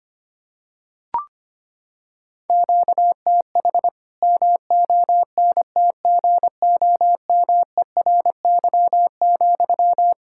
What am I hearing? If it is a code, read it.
Morse code, heard as a keyed tone, about 25 wpm, 700 Hz. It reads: QTH MONTGOMERY,
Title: Practice QSO - Exchange 1 - Beginning - 25wpm